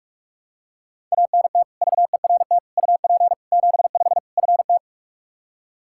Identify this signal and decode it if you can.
Morse code, heard as a keyed tone, about 46 wpm, 700 Hz. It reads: ANT VERT UP 75 FT
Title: Practice QSO - Exchange 2 - Info - 45wpm